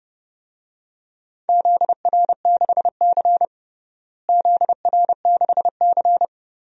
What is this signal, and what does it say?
Morse code, heard as a keyed tone, about 32 wpm, 700 Hz. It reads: ZR6C ZR6C
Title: DXCC-100 Call Signs - 4 Characters - Repeated 3x - 30wpm